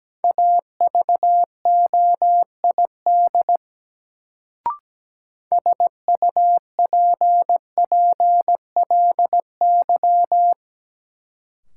Morse code, heard as a keyed tone, about 17 wpm, 700 Hz. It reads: AVOID SUPPLY
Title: Sentences from Top 1200 Words - New Word Drill Alt - 17wpm